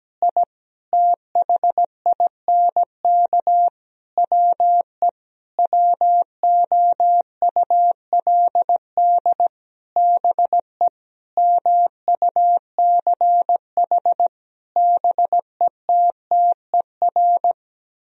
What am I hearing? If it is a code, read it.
Morse code, heard as a keyed tone, about 17 wpm, 700 Hz. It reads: I THINK WE WOULD BE MUCH BETTER
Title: Sentences from Top 1000 Words - Review Part 3 - 17wpm